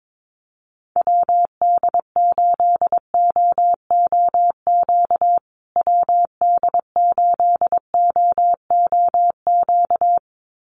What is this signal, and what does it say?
Morse code, heard as a keyed tone, about 22 wpm, 700 Hz. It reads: WD8OOQ WD8OOQ
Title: USA Call Signs Repeated 2x 22wpm